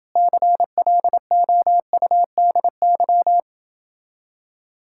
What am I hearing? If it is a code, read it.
Morse code, heard as a keyed tone, about 28 wpm, 700 Hz. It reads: CLOUDY